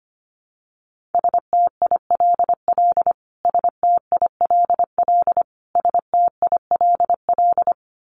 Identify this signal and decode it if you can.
Morse code, heard as a keyed tone, about 25 wpm, 700 Hz. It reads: STILL STILL STILL
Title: Top 500 Words Repeated 25wpm